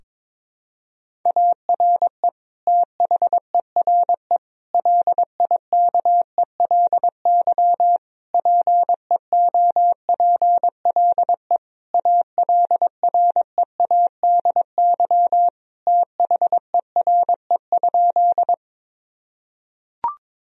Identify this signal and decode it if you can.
Morse code, heard as a keyed tone, about 22 wpm, 700 Hz. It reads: ARE THERE LIKELY PEOPLE ALREADY THERE?
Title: Sentences from Top 1000 Words - Review Part 4 - 22wpm